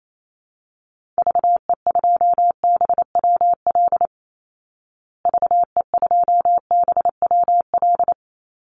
Morse code, heard as a keyed tone, about 28 wpm, 700 Hz. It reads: VE2BWL VE2BWL
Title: Call Signs Repeated 3x - 28wpm